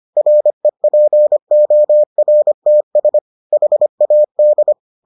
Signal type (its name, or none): Morse code